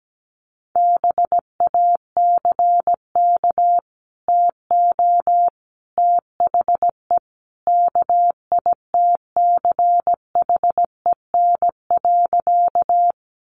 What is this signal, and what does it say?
Morse code, heard as a keyed tone, about 17 wpm, 700 Hz. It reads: BACK TO THE KITCHEN.
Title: Sentences from Top 1400 Words 17wpm